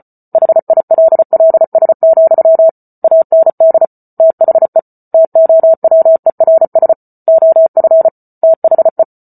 Morse code, heard as a keyed tone, about 34 wpm, 650 Hz. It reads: HILLS, AND THE TOWERS OF THE